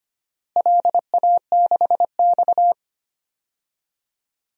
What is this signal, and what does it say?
Morse code, heard as a keyed tone, about 25 wpm, 700 Hz. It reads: LA6X
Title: DXCC-100 Call Signs - 4 Characters - Repeated 3x - 25wpm